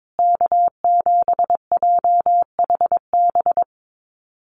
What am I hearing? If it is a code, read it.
Morse code, heard as a keyed tone, about 22 wpm, 700 Hz. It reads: K7JHB
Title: USA Call Signs Repeated 2x 22wpm